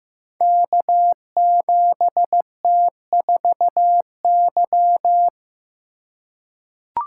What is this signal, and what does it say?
Morse code, heard as a keyed tone, about 15 wpm, 700 Hz. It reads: K7T4Y
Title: Code Groups - 5 Letter:Numbers - 15wpm